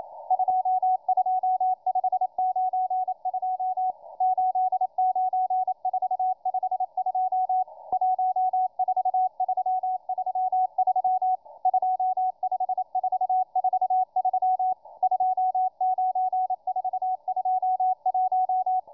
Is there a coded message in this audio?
Yes: Morse code from 0.3 to 18.8 s, heard as a keyed tone, about 28 wpm, 740 Hz, with radio noise. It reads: 22592 89452 14333 25443 29421